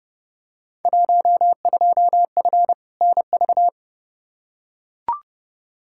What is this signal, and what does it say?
Morse code, heard as a keyed tone, about 30 wpm, 700 Hz. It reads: 12F NV
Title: Field Day Exchanges Encore - 30wpm